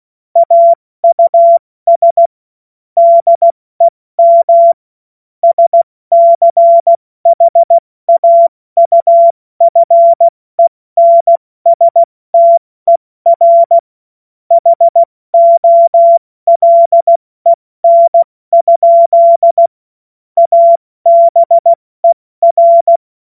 Morse code, heard as a keyed tone, about 16 wpm, 680 Hz. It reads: AUS DEM SCHAUFENSTER HOLEN? ABER
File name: Morsekurs / Witze / Witz 06 Februar 2024 70 F80